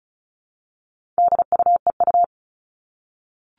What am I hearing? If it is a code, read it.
Morse code, heard as a keyed tone, about 34 wpm, 700 Hz. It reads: DUEU
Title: All Cut Numbers - 4-Digit 35wpm